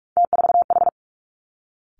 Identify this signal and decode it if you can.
Morse code, heard as a keyed tone, about 46 wpm, 700 Hz. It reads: T4H
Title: Three Challenging Letter-Number Combinations 45wpm